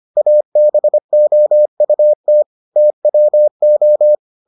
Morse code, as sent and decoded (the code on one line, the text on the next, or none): .- -... --- ..- - / - .-- ---
ABOUT TWO